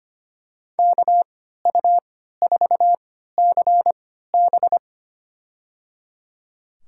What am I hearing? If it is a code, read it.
Morse code, heard as a keyed tone, about 25 wpm, spaced as for 14 wpm, 700 Hz. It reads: KU4CB
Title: USA Call Signs - 3x Character Spacing - 25wpm